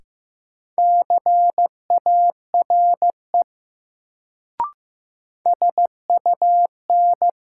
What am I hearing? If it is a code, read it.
Morse code, heard as a keyed tone, about 15 wpm, 700 Hz. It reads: CARE SUN
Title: Sentences from Top 300 Words - New Word Drill - 15wpm